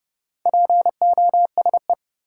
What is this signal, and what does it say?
Morse code, heard as a keyed tone, about 30 wpm, 700 Hz. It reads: POSE
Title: Sentences from Top 1000 Words - Review Part 2 - New Word Drill - 30wpm